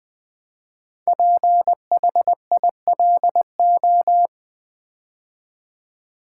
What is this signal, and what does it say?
Morse code, heard as a keyed tone, about 20 wpm, 700 Hz. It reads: PHILO
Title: Build A Word - 5 Letters At A Time - 10-Letter Words - 20wpm